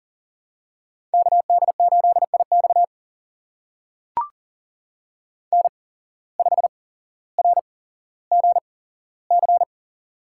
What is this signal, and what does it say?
Morse code, heard as a keyed tone, about 40 wpm, spaced as for 17 wpm, 700 Hz. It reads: KD8IX N5RGC
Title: USA Call Signs - 5 Characters - 8x Character Spacing - 40wpm